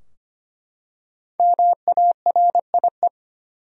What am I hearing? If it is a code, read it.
Morse code, heard as a keyed tone, about 25 wpm, 700 Hz. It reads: MARIE
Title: USA Common Names - Max 5 Letters - 25wpm